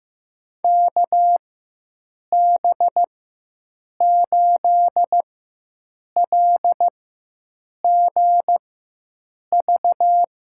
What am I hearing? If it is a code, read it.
Morse code, heard as a keyed tone, about 15 wpm, spaced as for 7 wpm, 700 Hz. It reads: KB8LGV
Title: USA Call Signs - 6 Characters - 4x Character Spacing - 15wpm